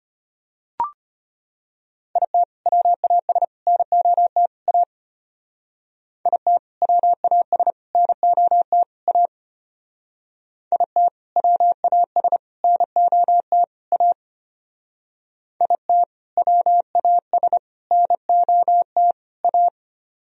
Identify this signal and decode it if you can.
Morse code, heard as a keyed tone, about 31 wpm, 700 Hz. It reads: IT WAS NOT A IT WAS NOT A IT WAS NOT A IT WAS NOT A
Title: Speed-Racing - Sets of 4 Words 25wpm